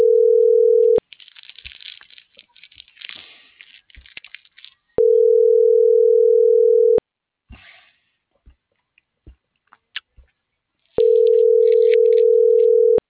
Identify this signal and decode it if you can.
Ringback tone (North America), 440 + 480 Hz.
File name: CENTRAL SAGOT / out omar.esagot